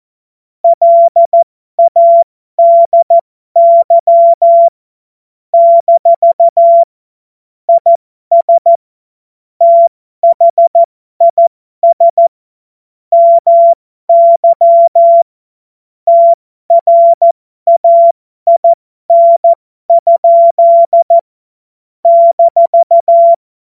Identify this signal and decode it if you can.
Morse code, heard as a keyed tone, about 14 wpm, spaced as for 12 wpm, 680 Hz. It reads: LADY - IS THIS MY TRAIN? -
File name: Morsekurs / Witze